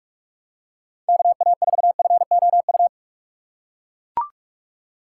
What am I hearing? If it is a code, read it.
Morse code, heard as a keyed tone, about 44 wpm, 700 Hz. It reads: KA4FOU